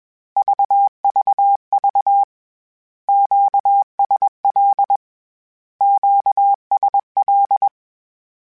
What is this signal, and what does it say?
Morse code, heard as a keyed tone, about 21 wpm, 800 Hz. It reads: VVV QSL QSL
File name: Q Calls / Q Calls-mp3s / Q Calls Twice @20